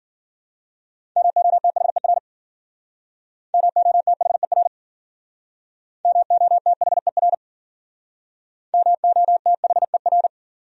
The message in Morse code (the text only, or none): MOTHER MOTHER MOTHER MOTHER